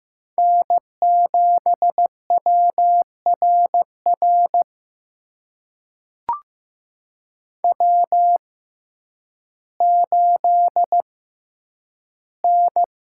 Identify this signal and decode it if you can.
Morse code, heard as a keyed tone, about 15 wpm, spaced as for 9 wpm, 700 Hz. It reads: N7WRR W8N
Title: USA Call Signs - 5 Characters - 6x Character Spacing - 15wpm